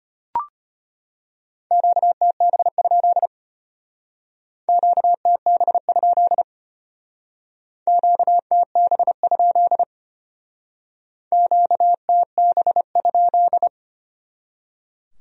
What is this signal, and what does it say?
Morse code, heard as a keyed tone, about 31 wpm, 700 Hz. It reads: QTB? QTB? QTB? QTB?